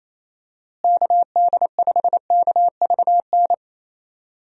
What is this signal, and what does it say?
Morse code, heard as a keyed tone, about 27 wpm, 700 Hz. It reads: KD5KVN
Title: USA Call Signs - 6 Characters - Repeated 3x - 28wpm